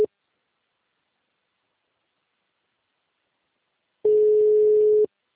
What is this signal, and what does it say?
Ringback tone (Europe, 425 Hz).